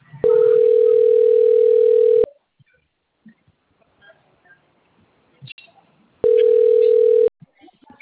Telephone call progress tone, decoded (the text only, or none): ringback tone